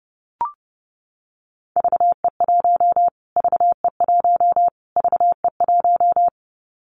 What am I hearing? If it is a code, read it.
Morse code, heard as a keyed tone, about 28 wpm, 690 Hz. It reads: VE1 VE1 VE1
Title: Callsign Prefixes for Top 25 DXCC - Repeated 3x - 30wpm